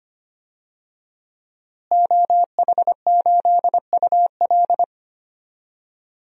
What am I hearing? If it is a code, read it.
Morse code, heard as a keyed tone, about 25 wpm, 700 Hz. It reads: OH8UL
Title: DXCC-100 Call Signs - 5 Characters - Repeated 2x - 25wpm